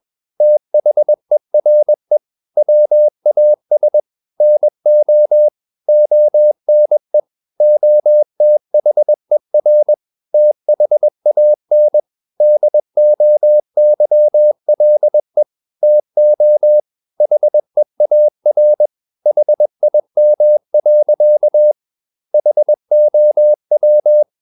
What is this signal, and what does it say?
Morse code, heard as a keyed tone, about 21 wpm, 600 Hz. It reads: THERE WAS NO ONE OTHER THAN DOYLE TO HEAR HIM. HOW
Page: CW med Gnister